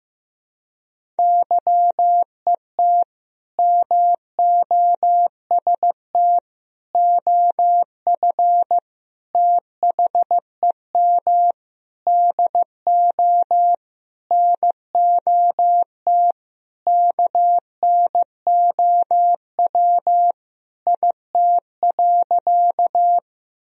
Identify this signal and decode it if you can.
Morse code, heard as a keyed tone, about 15 wpm, 700 Hz. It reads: YET MOST OF THEM DO NOT KNOW IT.
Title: Sentences from Top 1000 Words - Review Part 1 - 15wpm